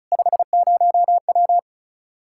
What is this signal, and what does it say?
Morse code, heard as a keyed tone, about 36 wpm, 700 Hz. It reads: 50W
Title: Practice QSO - Exchange 2 - Info - 35wpm